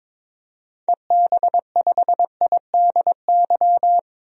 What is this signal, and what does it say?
Morse code, heard as a keyed tone, about 22 wpm, 700 Hz. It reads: EB5IDY
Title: DXCC-100 Call Signs - 6 Characters - 22wpm